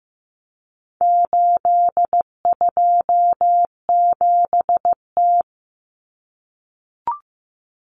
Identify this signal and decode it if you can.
Morse code, heard as a keyed tone, about 15 wpm, 700 Hz. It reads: 827T
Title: Common Cut Numbers - 4-Digit 15wpm